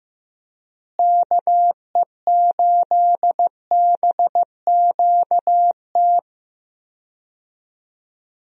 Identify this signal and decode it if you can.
Morse code, heard as a keyed tone, about 15 wpm, 700 Hz. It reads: KE8BQT
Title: USA Call Signs - 6 Characters - Repeated 2x - 15wpm